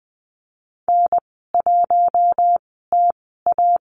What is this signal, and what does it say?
Morse code, heard as a keyed tone, about 20 wpm, spaced as for 15 wpm, 700 Hz. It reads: N1TA
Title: 088 Lesson 15g - Number 5 - Call Signs 20wpm